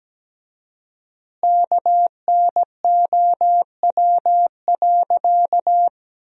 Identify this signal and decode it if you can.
Morse code, heard as a keyed tone, about 17 wpm, 700 Hz. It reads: KNOW.
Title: 4x Word-Spacing - Sentences from Top 100 Words - 17wpm